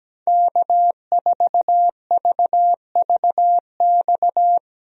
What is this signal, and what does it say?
Morse code, heard as a keyed tone, about 17 wpm, 700 Hz. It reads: K4VVX
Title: USA Call Signs - 5 Characters - Repeated 3x - 17wpm